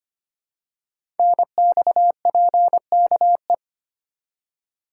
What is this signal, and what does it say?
Morse code, heard as a keyed tone, about 25 wpm, 700 Hz. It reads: NXPKE
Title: Code Groups - 5 Letters - 25wpm